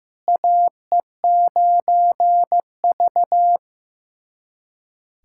Morse code, heard as a keyed tone, about 15 wpm, 700 Hz. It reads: AE9V
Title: USA Callsign DE Callsign - 4 Characters - 15wpm